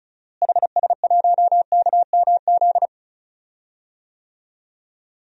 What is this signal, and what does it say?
Morse code, heard as a keyed tone, about 36 wpm, 700 Hz. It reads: HS1KMZ
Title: DXCC-100 Call Signs - 6 Characters - Repeated 2x - 35wpm